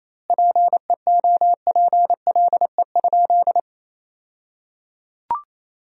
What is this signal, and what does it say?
Morse code, heard as a keyed tone, about 28 wpm, 700 Hz. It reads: PEOPLE?
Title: 2x Word-Spacing - Sentences from Top 200 Words - 28wpm